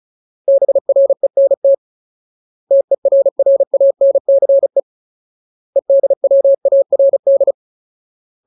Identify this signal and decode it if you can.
Morse code, heard as a keyed tone, about 35 wpm, 550 Hz. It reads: BRENT TERRANCE EDWARD
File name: Ultimate Word List / 35WPM Ultimate